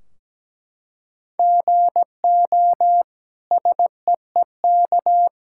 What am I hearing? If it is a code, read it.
Morse code, heard as a keyed tone, about 17 wpm, 700 Hz. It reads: GO SEEK
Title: Sentences from Top 1600 Words 17wpm